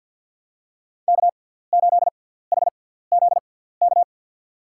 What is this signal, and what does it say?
Morse code, heard as a keyed tone, about 50 wpm, spaced as for 17 wpm, 700 Hz. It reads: K8HZK
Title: USA Call Signs - 6x Character Spacing - 50wpm